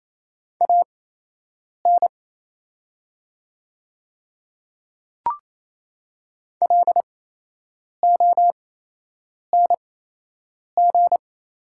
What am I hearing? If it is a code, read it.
Morse code, heard as a keyed tone, about 28 wpm, spaced as for 8 wpm, 700 Hz. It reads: AN LONG
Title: 8x Character Spacing - Top 100 Words - 28wpm